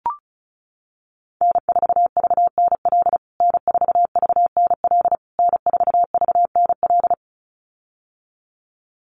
Morse code, heard as a keyed tone, about 34 wpm, 700 Hz. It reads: N4VNL N4VNL N4VNL
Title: USA Call Signs Repeated 3x 35wpm